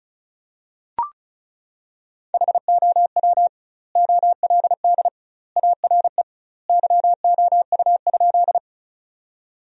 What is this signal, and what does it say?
Morse code, heard as a keyed tone, about 36 wpm, 700 Hz. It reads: HOW OLD ARE YOU?
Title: 2x Word-Spacing - Sentences from Top 200 Words - 35wpm